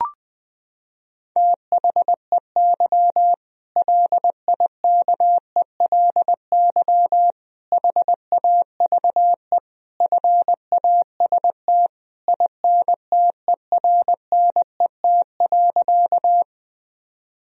Morse code, heard as a keyed tone, about 21 wpm, 700 Hz. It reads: THEY LIKELY HAVE FAST INTERNET.
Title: Sentences from Top 1000 Words - Review Part 4 - 20wpm